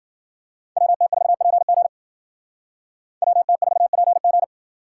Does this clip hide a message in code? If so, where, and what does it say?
Yes: Morse code from 0.8 to 4.4 s, heard as a keyed tone, about 52 wpm, 700 Hz. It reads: WT4PG WT4PG